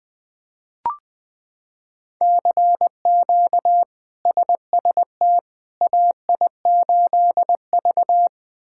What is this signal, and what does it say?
Morse code, heard as a keyed tone, about 20 wpm, 700 Hz. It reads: CQ SST AI8V
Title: K1USN SST QSO Practice - 20wpm